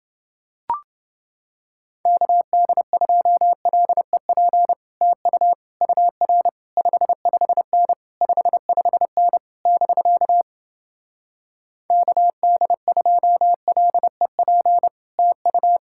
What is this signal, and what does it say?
Morse code, heard as a keyed tone, about 30 wpm, 700 Hz. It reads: KD2LEP TU UR 55N 55N <BK> KD2LEP TU